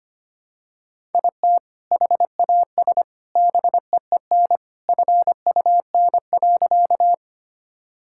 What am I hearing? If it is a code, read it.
Morse code, heard as a keyed tone, about 25 wpm, 700 Hz. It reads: IT HAS BEEN FUN.
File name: practice / 25wpm / Sentences with 3-Word Phrases - 25wpm